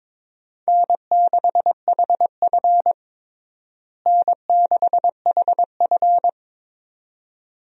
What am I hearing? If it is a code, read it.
Morse code, heard as a keyed tone, about 22 wpm, 700 Hz. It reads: N6HF N6HF
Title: USA Call Signs - 4 Characters - Repeated 2x - 22wpm